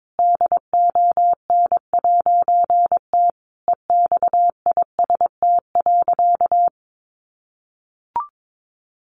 Morse code, heard as a keyed tone, about 22 wpm, 700 Hz. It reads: DON'T EXIST.
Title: A Sound Of Thunder 22wpm